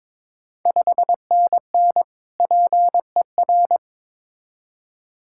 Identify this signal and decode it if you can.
Morse code, heard as a keyed tone, about 22 wpm, 700 Hz. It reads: 5NN PER
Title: Ontario QSO Party Exchange 22wpm